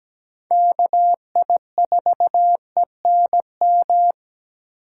Morse code, heard as a keyed tone, about 17 wpm, 700 Hz. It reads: KI4ENM
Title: USA Call Signs - 6 Characters - 2x Character Spacing - 17wpm